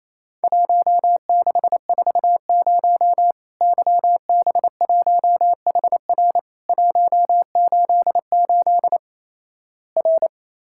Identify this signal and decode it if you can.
Morse code, heard as a keyed tone, about 28 wpm, 700 Hz. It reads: 1640 YB1HR 188 R
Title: Worked All Europe DX Contest - QTC Exchange - 28wpm